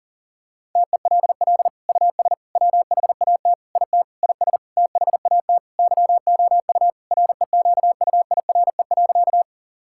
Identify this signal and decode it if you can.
Morse code, heard as a keyed tone, about 37 wpm, 700 Hz. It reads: TELL US WHAT IT IS THAT YOU REQUIRE.
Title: Sentences from Top 1000 Words - Review Part 6 - 40wpm